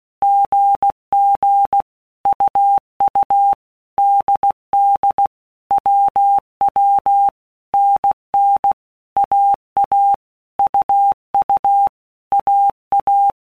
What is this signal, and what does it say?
Morse code, heard as a keyed tone, about 16 wpm, 800 Hz. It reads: GG UU DD WW NN AA UU AA
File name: All Groups 1-minute Blocks / Groups@15 / AWUNDG